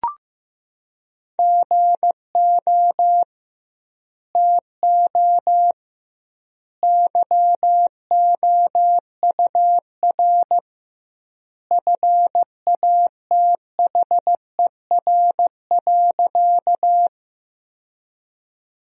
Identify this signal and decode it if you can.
Morse code, heard as a keyed tone, about 15 wpm, 700 Hz. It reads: GO TO YOUR FATHER.
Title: 2x Word-Spacing - Sentences from Top 300 Words - 15wpm